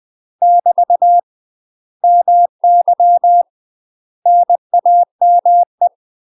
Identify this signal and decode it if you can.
Morse code, heard as a keyed tone, about 20 wpm, 700 Hz. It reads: <BT> MY NAME
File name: UK MASTER QSOs TRAINING / 20WPM / UK MASTER QSOs 0062